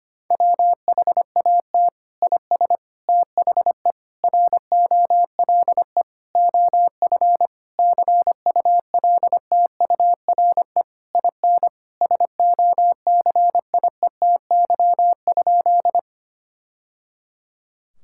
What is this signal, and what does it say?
Morse code, heard as a keyed tone, about 25 wpm, 700 Hz. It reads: WHAT IS THE ROLE OF CULTURE IN SOCIETY?
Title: Sentences from Top 1000 Words - Review Part 6 - 25wpm